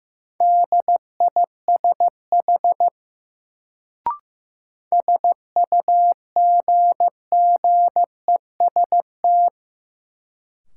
Morse code, heard as a keyed tone, about 15 wpm, 700 Hz. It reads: DISH SUGGEST
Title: Sentences from Top 1300 Words - New Word Drill - 15wpm